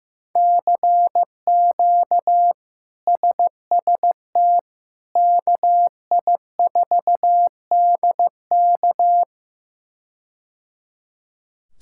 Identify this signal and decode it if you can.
Morse code, heard as a keyed tone, about 15 wpm, 700 Hz. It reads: CQ SST KI4DK
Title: K1USN SST QSO Practice - 15wpm